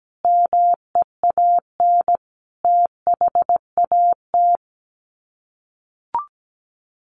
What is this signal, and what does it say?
Morse code, heard as a keyed tone, about 17 wpm, 700 Hz. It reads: MEAN THAT
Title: Sets of 5 Words Encore2 17wpm